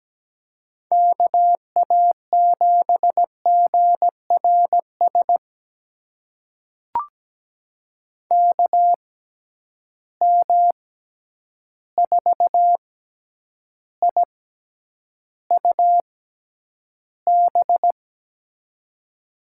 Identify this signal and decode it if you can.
Morse code, heard as a keyed tone, about 17 wpm, spaced as for 9 wpm, 700 Hz. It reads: KA7GRS KM4IUB
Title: USA Call Signs - 6 Characters - 6x Character Spacing - 17wpm